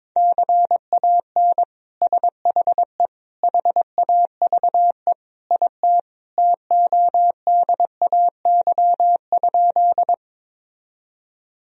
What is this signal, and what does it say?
Morse code, heard as a keyed tone, about 22 wpm, 700 Hz. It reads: CAN SHE HAVE IT TODAY?
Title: Sentences from Top 1000 Words - Review Part 4 - 22wpm